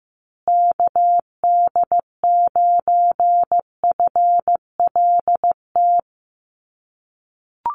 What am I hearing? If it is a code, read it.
Morse code, heard as a keyed tone, about 15 wpm, 700 Hz. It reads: KD9FLT
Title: USA Call Signs Repeated 3x 15wpm